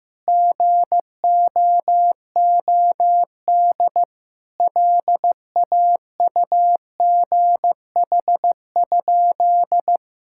Morse code, heard as a keyed tone, about 15 wpm, 700 Hz. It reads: GOOD LAUGH?
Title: Sentences from Top 1000 Words - Review Part 5 - 15wpm